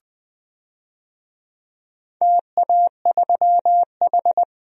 Morse code, heard as a keyed tone, about 20 wpm, 690 Hz. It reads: TA3H
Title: DXCC-100 Call Signs - 4 Characters - Repeated 3x - 20wpm